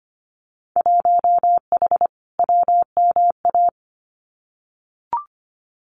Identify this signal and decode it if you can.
Morse code, heard as a keyed tone, about 25 wpm, 700 Hz. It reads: 1H WMA
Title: Winter Field Day Exchanges 25wpm